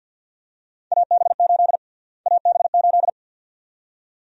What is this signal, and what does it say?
Morse code, heard as a keyed tone, about 45 wpm, 700 Hz. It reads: AB8 AB8
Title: USA Call Sign Prefixes - Repeated 2x - 50wpm